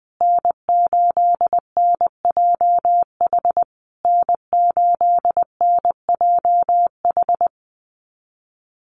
Morse code, heard as a keyed tone, about 20 wpm, 700 Hz. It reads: N8NJH N8NJH
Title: USA Call Signs Repeated 3x 20wpm